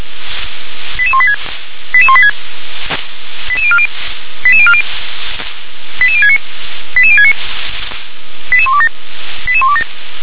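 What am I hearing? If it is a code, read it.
Five-tone selective call (ZVEI), heard as quick runs of five tones, tones of about 70 ms. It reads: 88186 88186 88040 88040 88069 88069 88127 88127
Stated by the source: ZVEI five-tone selective call received with an SDR.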